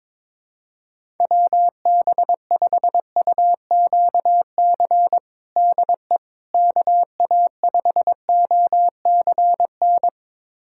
Morse code, heard as a keyed tone, about 22 wpm, 700 Hz. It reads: WB5UQC DE KA5OCN
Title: USA Callsign DE Callsign - 6 Characters - 22wpm